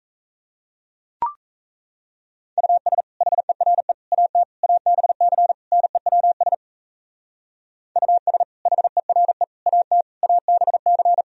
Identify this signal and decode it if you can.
Morse code, heard as a keyed tone, about 34 wpm, 700 Hz. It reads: US HERE AT ABC NEWS US HERE AT ABC
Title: Speed-Racing - Sets of 5 Words 28wpm